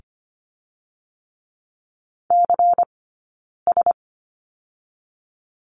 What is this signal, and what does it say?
Morse code, heard as a keyed tone, about 24 wpm, 700 Hz. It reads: C S
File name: course / 25:10wpm / 092 Lesson 16d - Letter c - Two Characters 25wpm